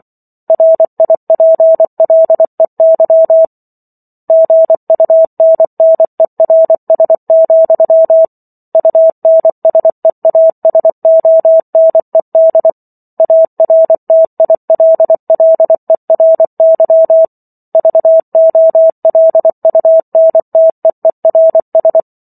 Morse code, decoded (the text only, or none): RIPLEY GUNNERS, UNSEASONED ARTILLERY VOLUNTEERS